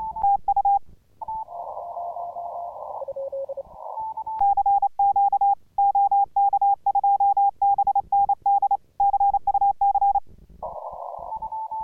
Morse code, heard as a keyed tone, about 28 wpm, 800 Hz. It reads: TU CQ OK2BND CUC